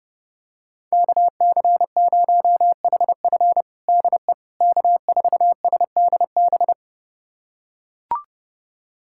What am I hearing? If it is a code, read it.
Morse code, heard as a keyed tone, about 30 wpm, 700 Hz. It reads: KC0HF DE K4SDB
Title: USA Callsign DE Callsign - 5 Characters - 30wpm